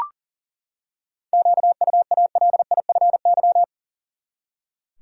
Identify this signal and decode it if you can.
Morse code, heard as a keyed tone, about 37 wpm, 700 Hz. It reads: QUALIFY
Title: Seven Letter Words - 40wpm